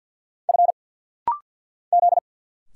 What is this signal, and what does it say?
Morse code, heard as a keyed tone, about 50 wpm, 700 Hz. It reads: F Z
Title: Single Letter-Number - Rapid-Fire 50wpm